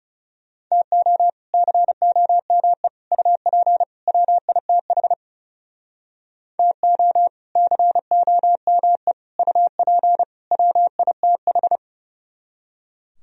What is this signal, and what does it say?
Morse code, heard as a keyed tone, about 36 wpm, 700 Hz. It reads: TO COME UP WITH TO COME UP WITH
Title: Speed-Racing - Sets of 4 Words 30wpm